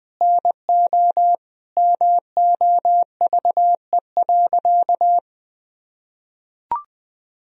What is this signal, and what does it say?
Morse code, heard as a keyed tone, about 20 wpm, 700 Hz. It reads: NO MOVE.
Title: Sentences from Top 1000 Words - Review Part 1 - 20wpm